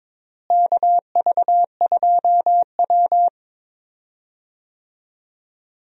Morse code, heard as a keyed tone, about 22 wpm, 700 Hz. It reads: KV2W
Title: USA Call Signs - 4 Characters - Repeated 2x - 22wpm